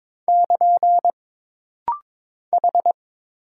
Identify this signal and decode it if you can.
Morse code, heard as a keyed tone, about 22 wpm, 700 Hz. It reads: <KN> H
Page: This rapid-fire timing is geared to push your ICR (Instant Character Recognition) skills.